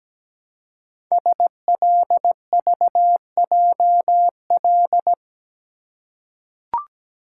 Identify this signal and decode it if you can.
Morse code, heard as a keyed tone, about 17 wpm, 700 Hz. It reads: SLVJL
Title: Code Groups - 5 Letters - 17wpm